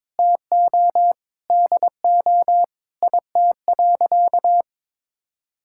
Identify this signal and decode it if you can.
Morse code, heard as a keyed tone, about 22 wpm, 700 Hz. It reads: TO DO IT.
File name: practice / 22wpm / Sentences with 8-Word Phrases - 22wpm